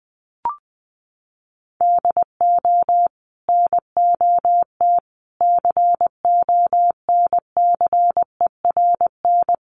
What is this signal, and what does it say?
Morse code, heard as a keyed tone, about 20 wpm, 700 Hz. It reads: DO NOT CONCERN
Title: Sentences from Top 1000 Words 20wpm